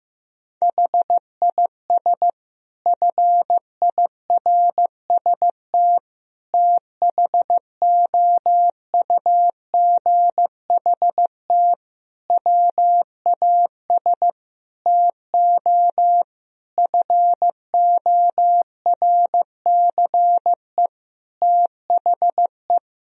Morse code, heard as a keyed tone, about 15 wpm, 700 Hz. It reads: HIS FIRST THOUGHT WAS TO FORCE THE
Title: Sentences from Top 1000 Words - Review Part 3 - 15wpm